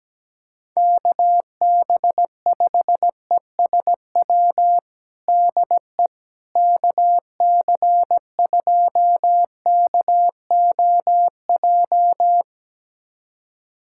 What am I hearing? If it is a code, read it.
Morse code, heard as a keyed tone, about 17 wpm, 700 Hz. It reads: KB5ESW DE KC2KOJ
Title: USA Callsign DE Callsign 17wpm